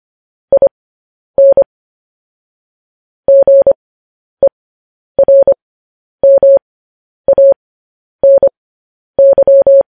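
Morse code, heard as a keyed tone, about 25 wpm, spaced as for 10 wpm, 550 Hz. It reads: IN GERMANY
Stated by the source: Morse code news headlines.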